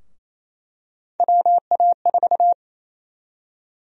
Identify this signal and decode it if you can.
Morse code, heard as a keyed tone, about 27 wpm, 700 Hz. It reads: WA4
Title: USA Call Sign Prefixes - Repeated 2x - 28wpm